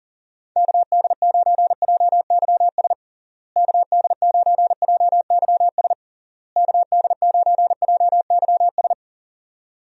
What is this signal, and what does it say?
Morse code, heard as a keyed tone, about 40 wpm, 700 Hz. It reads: KD9JYS KD9JYS KD9JYS
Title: USA Call Signs - 6 Characters - Repeated 3x - 40wpm